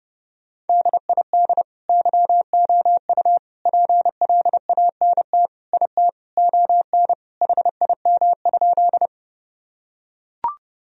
Morse code, heard as a keyed tone, about 32 wpm, 700 Hz. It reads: DID YOU PLANT IT ON HIM?
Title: Sentences from Top 1000 Words - Review Part 1 - 30wpm